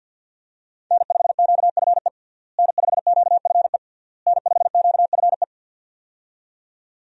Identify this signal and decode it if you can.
Morse code, heard as a keyed tone, about 48 wpm, 700 Hz. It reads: N5QFE N5QFE N5QFE
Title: USA Call Signs - 5 Characters - Repeated 3x - 50wpm